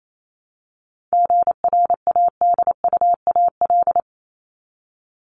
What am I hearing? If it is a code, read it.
Morse code, heard as a keyed tone, about 28 wpm, 700 Hz. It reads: GRADUAL
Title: LY Suffix - Top 100 Words with Base Words 28wpm